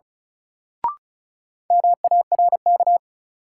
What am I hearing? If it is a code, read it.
Morse code, heard as a keyed tone, about 35 wpm, 700 Hz. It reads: MARK
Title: Sentences from Top 300 Words - New Word Drill - 35wpm